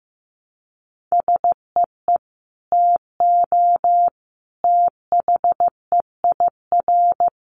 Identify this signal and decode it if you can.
Morse code, heard as a keyed tone, about 15 wpm, 700 Hz. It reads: SEE TO THEIR
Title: Sentences from Top 1100 Words 15wpm